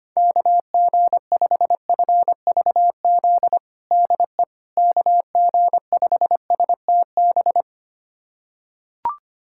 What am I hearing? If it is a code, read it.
Morse code, heard as a keyed tone, about 25 wpm, 700 Hz. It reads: KG5FVZ DE KG5STB
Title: USA Callsign DE Callsign - 6 Characters - 25wpm